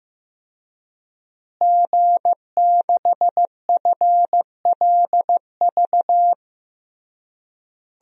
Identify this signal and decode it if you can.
Morse code, heard as a keyed tone, about 15 wpm, 700 Hz. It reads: G6FLV
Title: DXCC-100 Call Signs - 5 Characters - Repeated 3x - 15wpm